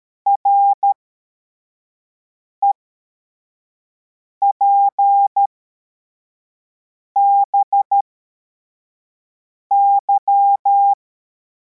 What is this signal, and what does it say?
Morse code, heard as a keyed tone, about 13 wpm, spaced as for 4 wpm, 800 Hz. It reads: REPBY
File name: formation / cw / Text73